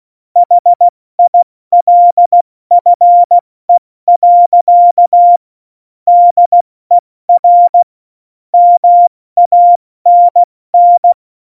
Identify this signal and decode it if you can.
Morse code, heard as a keyed tone, about 16 wpm, 700 Hz. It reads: HILFE. DER MANN